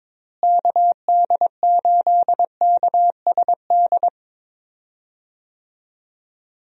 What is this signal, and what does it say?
Morse code, heard as a keyed tone, about 22 wpm, 700 Hz. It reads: KD8KSD
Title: USA Call Signs - 6 Characters - Repeated 2x - 22wpm